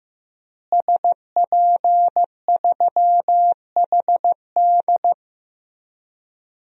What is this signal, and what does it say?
Morse code, heard as a keyed tone, about 15 wpm, 700 Hz. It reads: SP3HD
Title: DXCC-100 Call Signs - 5 Characters - Repeated 3x - 15wpm